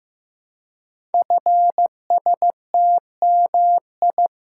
Morse code, heard as a keyed tone, about 15 wpm, 700 Hz. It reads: FSTMI
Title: Code Groups - 5 Letters - 15wpm